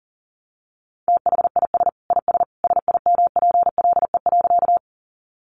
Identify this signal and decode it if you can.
Morse code, heard as a keyed tone, about 40 wpm, 700 Hz. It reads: THIS IS SIMPLE.
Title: Sentences from Top 400 Words 40wpm